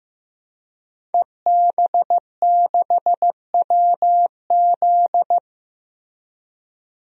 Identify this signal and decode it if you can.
Morse code, heard as a keyed tone, about 15 wpm, 700 Hz. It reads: EB6WZ
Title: DXCC-100 Call Signs - 5 Characters - Repeated 3x - 15wpm